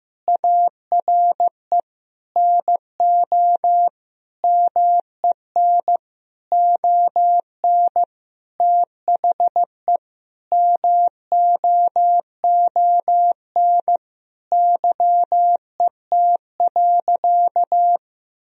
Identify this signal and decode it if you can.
Morse code, heard as a keyed tone, about 15 wpm, 700 Hz. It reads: ARE NO MEN ON THE MOON YET.
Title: Sentences from Top 1000 Words - Review Part 3 - 15wpm